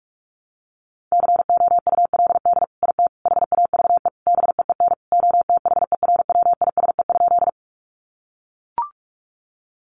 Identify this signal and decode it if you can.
Morse code, heard as a keyed tone, about 44 wpm, 700 Hz. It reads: COULD IT HAVE BEEN OTHERWISE?
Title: Sentences from Top 1200 Words 45wpm